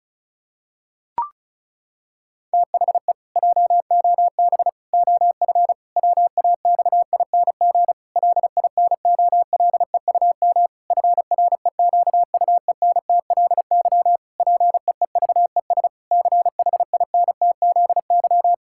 Morse code, heard as a keyed tone, about 36 wpm, 700 Hz. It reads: THE JOB OF WAXING LINOLEUM FREQUENTLY PEEVES CHINTZY
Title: Pangram Challenge v2 - 35wpm